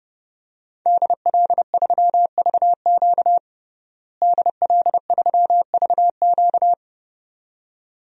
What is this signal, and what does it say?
Morse code, heard as a keyed tone, about 30 wpm, 700 Hz. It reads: DL3VQ DL3VQ
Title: DXCC-100 Call Signs - 5 Characters - Repeated 2x - 30wpm